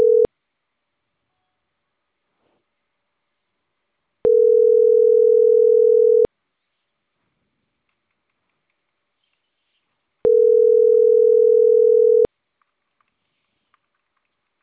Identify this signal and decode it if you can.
Ringback tone (North America), 440 + 480 Hz.